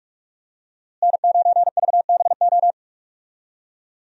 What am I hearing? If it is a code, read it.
Morse code, heard as a keyed tone, about 46 wpm, 700 Hz. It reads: N9VBO
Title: USA Call Signs - 5 Characters - Repeated 3x - 45wpm